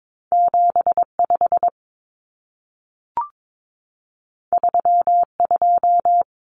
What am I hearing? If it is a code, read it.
Morse code, heard as a keyed tone, about 22 wpm, 700 Hz. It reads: 75 32
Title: Common Cut Numbers - 2-Digit 22wpm